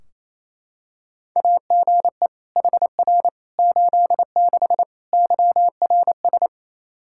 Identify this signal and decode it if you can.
Morse code, heard as a keyed tone, about 28 wpm, 700 Hz. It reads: AGE HR 86 YRS
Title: Practice QSO - Exchange 3 - More Info - 28wpm